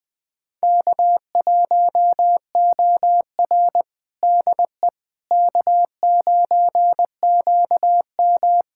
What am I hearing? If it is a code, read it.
Morse code, heard as a keyed tone, about 21 wpm, 700 Hz. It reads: K1OR DE K9QM
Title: USA Callsign DE Callsign - 4 Characters - 20wpm